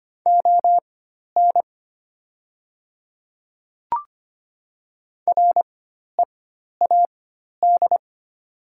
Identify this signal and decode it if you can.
Morse code, heard as a keyed tone, about 25 wpm, spaced as for 12 wpm, 700 Hz. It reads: ON READ
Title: 4x Character Spacing - Top 200 Words - 25wpm